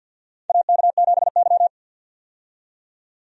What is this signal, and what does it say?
Morse code, heard as a keyed tone, about 53 wpm, 700 Hz. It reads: AK7Y
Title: USA Call Signs - 4 Characters - 50wpm